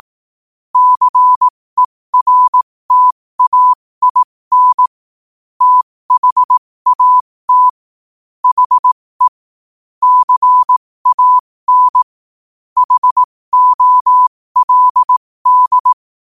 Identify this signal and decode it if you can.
Morse code, heard as a keyed tone, about 18 wpm, spaced as for 15 wpm, 1000 Hz. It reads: CERTAIN THAT HE CAN HOLD
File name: qotdfiles / 15wpm